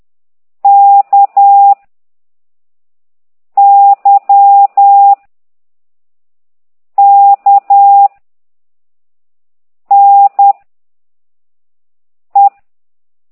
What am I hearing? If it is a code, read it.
Morse code, heard as a keyed tone, about 10 wpm, 800 Hz. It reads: K Y K N E
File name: exerc7